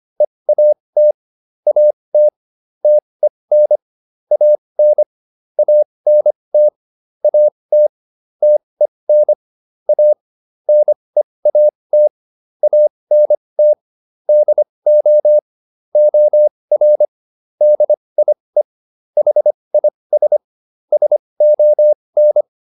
Morse code, heard as a keyed tone, about 25 wpm, spaced as for 20 wpm, 600 Hz. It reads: EAT AT TEN AN ANT AT TEN A NEAT ANT DO OR DIE HIS SON